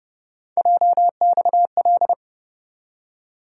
Morse code, heard as a keyed tone, about 30 wpm, 700 Hz. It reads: JXL
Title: USA Call Sign Postfixes - 3 Letters 30wpm